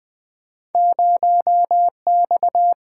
Morse code, heard as a keyed tone, about 20 wpm, 700 Hz. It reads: 0X
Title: Single-Letter Single-Number 20wpm